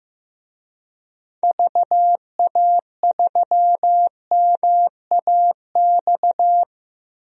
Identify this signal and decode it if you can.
Morse code, heard as a keyed tone, about 15 wpm, 700 Hz. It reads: VA3MAX
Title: DXCC-100 Call Signs - 6 Characters - Repeated 2x - 15wpm